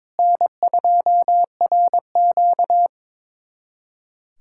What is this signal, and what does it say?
Morse code, heard as a keyed tone, about 22 wpm, 700 Hz. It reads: N2RQ
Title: USA Callsign DE Callsign 22wpm